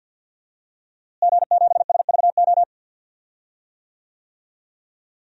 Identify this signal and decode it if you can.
Morse code, heard as a keyed tone, about 50 wpm, 700 Hz. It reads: G7SVO
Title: DXCC-100 Call Signs - 5 Characters - Repeated 3x - 50wpm